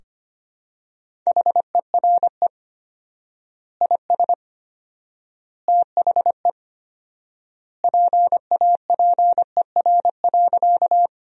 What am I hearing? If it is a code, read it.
Morse code, heard as a keyed tone, about 25 wpm, 700 Hz. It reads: HERE IS THE PAPER.
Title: 4x Word-Spacing - Sentences from Top 300 Words - 25wpm